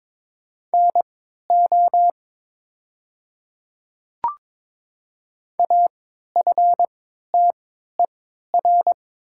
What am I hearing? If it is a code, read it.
Morse code, heard as a keyed tone, about 22 wpm, spaced as for 12 wpm, 700 Hz. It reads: NO AFTER
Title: 3x Character Spacing - Top 200 Words - 22wpm